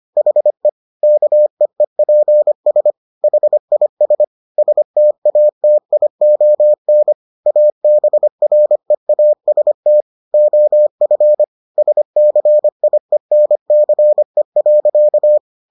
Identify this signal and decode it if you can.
Morse code, heard as a keyed tone, about 25 wpm, 600 Hz. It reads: HE KEEPS HIS STATION ABREAST OF SCIENCE.